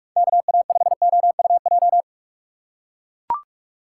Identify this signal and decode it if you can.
Morse code, heard as a keyed tone, about 41 wpm, 700 Hz. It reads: KA5OUJ
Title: USA Call Signs - 6 Characters - 2x Character Spacing - 45wpm